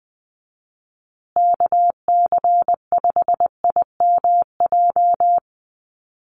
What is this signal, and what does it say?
Morse code, heard as a keyed tone, about 20 wpm, 700 Hz. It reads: KC5IMJ
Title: USA Call Signs Repeated 3x 20wpm